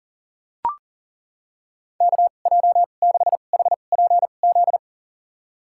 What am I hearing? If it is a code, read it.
Morse code, heard as a keyed tone, about 40 wpm, spaced as for 29 wpm, 700 Hz. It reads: KJ6HPZ
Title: USA Call Signs - 6 Characters - 2x Character Spacing - 40wpm